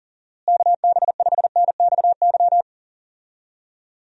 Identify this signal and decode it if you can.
Morse code, heard as a keyed tone, about 40 wpm, 700 Hz. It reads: KB5NXY
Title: USA Call Signs - 6 Characters - Repeated 3x - 40wpm